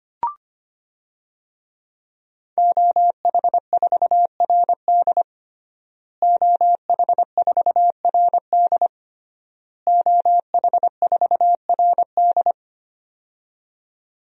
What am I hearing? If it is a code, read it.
Morse code, heard as a keyed tone, about 25 wpm, 700 Hz. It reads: OH4RD OH4RD OH4RD
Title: DXCC-100 Call Signs - 5 Characters - Repeated 3x - 25wpm